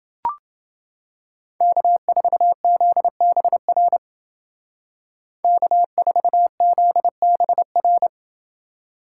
Morse code, heard as a keyed tone, about 28 wpm, 700 Hz. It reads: K4ZBR K4ZBR